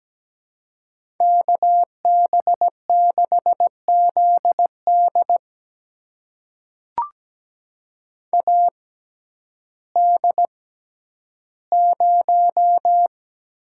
Morse code, heard as a keyed tone, about 17 wpm, spaced as for 10 wpm, 700 Hz. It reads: KB6ZD AD0
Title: USA Call Signs - 5 Characters - 6x Character Spacing - 17wpm